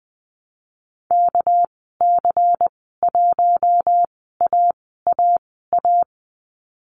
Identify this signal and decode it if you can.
Morse code, heard as a keyed tone, about 20 wpm, spaced as for 15 wpm, 700 Hz. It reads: KC1AAA
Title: 213 Lesson 34g - Number 8 - Call Signs 20wpm